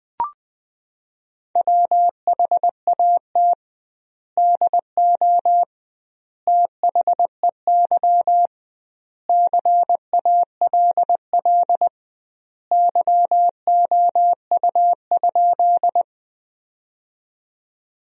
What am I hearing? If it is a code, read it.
Morse code, heard as a keyed tone, about 20 wpm, 700 Hz. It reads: WHAT DO THEY CALL YOU?
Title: 2x Word-Spacing - Sentences from Top 100 Words - 20wpm